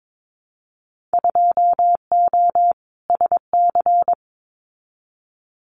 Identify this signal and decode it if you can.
Morse code, heard as a keyed tone, about 22 wpm, 700 Hz. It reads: 2O SC
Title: Winter Field Day Exchanges 22wpm